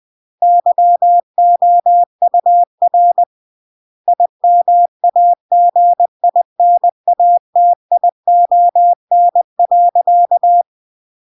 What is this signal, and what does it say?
Morse code, heard as a keyed tone, about 20 wpm, 700 Hz. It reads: YOUR IMAGINATION.
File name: CW / jodorowsky / 82 commandments 20-20